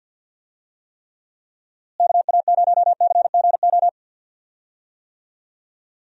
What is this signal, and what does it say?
Morse code, heard as a keyed tone, about 52 wpm, 700 Hz. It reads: KA0CGO
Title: USA Call Signs - 6 Characters - Repeated 3x - 50wpm